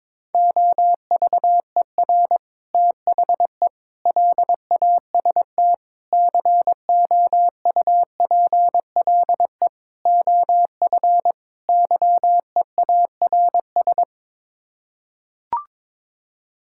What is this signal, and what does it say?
Morse code, heard as a keyed tone, about 22 wpm, 700 Hz. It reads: OVER THE LAST COUPLE OF YEARS
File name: practice / 22wpm / Sets of 6 Words - 22wpm